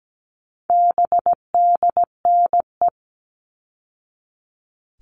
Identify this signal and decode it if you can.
Morse code, heard as a keyed tone, about 17 wpm, 700 Hz. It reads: BDNE
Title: All Cut Numbers - 4-Digit 17wpm